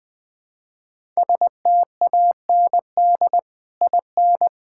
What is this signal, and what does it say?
Morse code, heard as a keyed tone, about 20 wpm, 700 Hz. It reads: STAND IN
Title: Sentences with 2-Word Phrases - Spoken Beginning - 20wpm